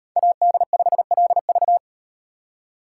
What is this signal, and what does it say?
Morse code, heard as a keyed tone, about 38 wpm, 700 Hz. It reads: AD5LV